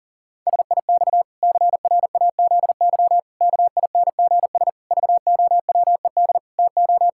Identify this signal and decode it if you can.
Morse code, heard as a keyed tone, about 40 wpm, 700 Hz. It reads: SIX CRAZY KINGS VOWED TO
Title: Pangram Challenge v2 - 40wpm